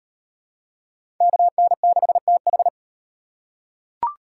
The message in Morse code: KN6TH

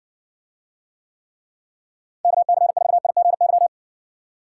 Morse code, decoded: KC4IGY

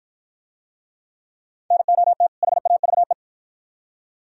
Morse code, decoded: NOT HAVE

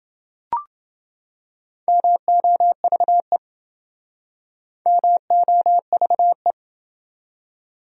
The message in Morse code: MOVE MOVE

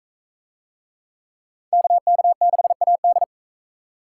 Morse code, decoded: KK6AD

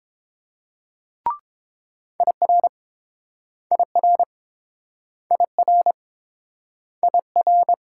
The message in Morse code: IR IR IR IR

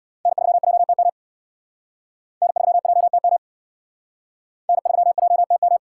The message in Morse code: N3JTM N3JTM N3JTM

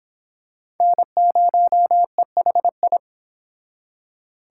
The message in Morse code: N0EHI

